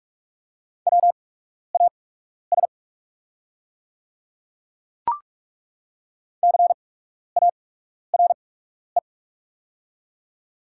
Morse code, heard as a keyed tone, about 47 wpm, spaced as for 12 wpm, 690 Hz. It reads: WAS CARE
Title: 8x Character Spacing - Top 300 Words - 45wpm